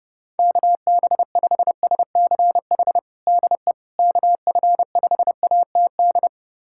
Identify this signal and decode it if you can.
Morse code, heard as a keyed tone, about 30 wpm, 700 Hz. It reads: KB5SCH DE KF5ATD
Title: USA Callsign DE Callsign - 6 Characters - 30wpm